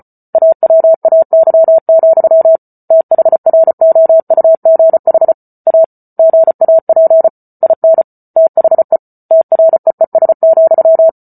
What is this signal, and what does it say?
Morse code, heard as a keyed tone, about 34 wpm, 650 Hz. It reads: AWAY, THROUGH A GAP IN THE TREES,